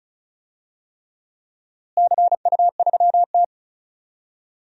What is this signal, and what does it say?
Morse code, heard as a keyed tone, about 36 wpm, 700 Hz. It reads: CU3T
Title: DXCC-100 Call Signs - 4 Characters - Repeated 3x - 35wpm